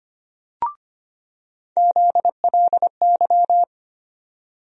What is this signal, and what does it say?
Morse code, heard as a keyed tone, about 25 wpm, 700 Hz. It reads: ZLY
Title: USA Call Sign Postfixes - 3 Letters 25wpm